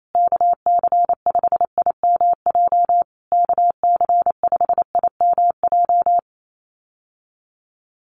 Morse code, heard as a keyed tone, about 28 wpm, 700 Hz. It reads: KC5IMJ KC5IMJ
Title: USA Call Signs Repeated 3x 28wpm